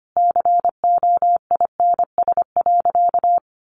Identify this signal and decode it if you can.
Morse code, heard as a keyed tone, about 25 wpm, 700 Hz. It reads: COINS.
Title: A Sound Of Thunder 25wpm